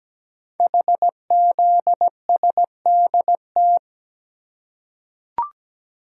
Morse code, heard as a keyed tone, about 17 wpm, 700 Hz. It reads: HZSDT
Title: Code Groups - 5 Letters - 17wpm